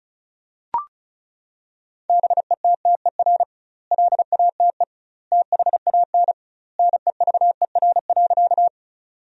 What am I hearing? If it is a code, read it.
Morse code, heard as a keyed tone, about 36 wpm, 700 Hz. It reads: BETTER LATE THAN NEVER.
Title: 2x Word-Spacing - Sentences from Top 400 Words - 35wpm